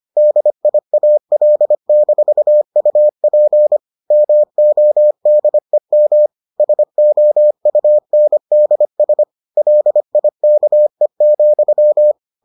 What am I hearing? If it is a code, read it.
Morse code, heard as a keyed tone, about 25 wpm, 600 Hz. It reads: DIAL-UP MODEM SOUNDS LIKE,